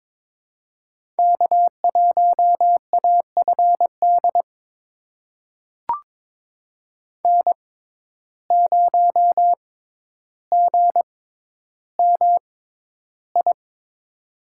Morse code, heard as a keyed tone, about 23 wpm, spaced as for 11 wpm, 700 Hz. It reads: K1AFD N0GMI
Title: USA Call Signs - 6x Character Spacing - 22wpm